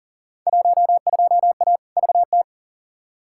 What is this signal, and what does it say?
Morse code, heard as a keyed tone, about 40 wpm, 700 Hz. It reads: 12A VT
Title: Field Day Exchanges Encore - 40wpm